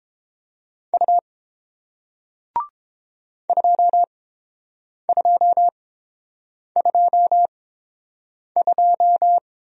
Morse code, heard as a keyed tone, about 30 wpm, spaced as for 7 wpm, 700 Hz. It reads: U 2222